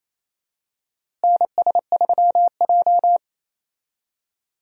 Morse code, heard as a keyed tone, about 28 wpm, 700 Hz. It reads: NS3J
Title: USA Call Signs - 4 Characters - Repeated 3x - 28wpm